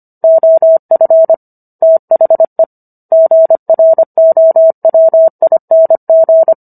Morse code, heard as a keyed tone, about 25 wpm, 650 Hz. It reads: OF THE GROWING
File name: Book One / 16-Chapter 16 - 25 WPM